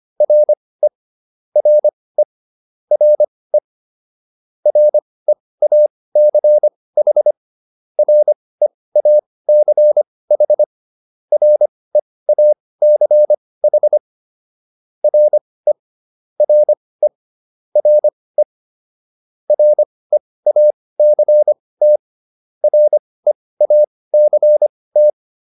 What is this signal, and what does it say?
Morse code, heard as a keyed tone, about 25 wpm, spaced as for 18 wpm, 600 Hz. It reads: RE RE RE REACH REACH REACH RE RE RE REACT REACT